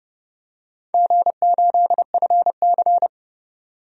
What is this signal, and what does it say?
Morse code, heard as a keyed tone, about 30 wpm, 700 Hz. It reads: G8FC
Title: DXCC-100 Call Signs - 4 Characters - 30wpm